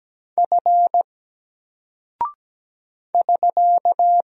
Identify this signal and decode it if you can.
Morse code, heard as a keyed tone, about 17 wpm, 700 Hz. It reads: F <SK>
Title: Single Character (Letter:Number:Punctuation:Prosign) 17wpm